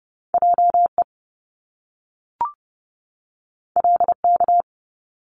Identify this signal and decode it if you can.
Morse code, heard as a keyed tone, about 30 wpm, 700 Hz. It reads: J E L K
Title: Top Two Letter Combinations in Words 30wpm